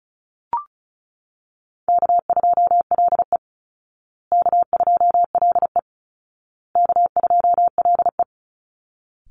Morse code, heard as a keyed tone, about 34 wpm, 700 Hz. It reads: K2LE K2LE K2LE
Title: Easy Call Signs Repeated 3x - 35wpm